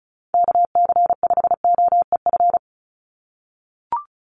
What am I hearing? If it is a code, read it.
Morse code, heard as a keyed tone, about 34 wpm, 700 Hz. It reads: KC5OEF
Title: USA Call Signs 35wpm